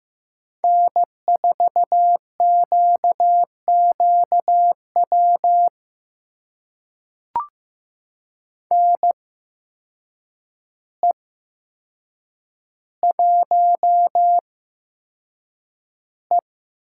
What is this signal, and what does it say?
Morse code, heard as a keyed tone, about 15 wpm, spaced as for 7 wpm, 700 Hz. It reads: N4QQW NE1E
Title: USA Call Signs - 5 Characters - 8x Character Spacing - 15wpm